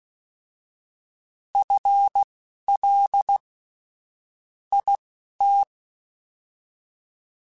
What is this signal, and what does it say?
Morse code, heard as a keyed tone, about 16 wpm, spaced as for 11 wpm, 800 Hz. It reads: FL IT
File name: FL / FL mp3s / FL Prefixes 3x2 sp @10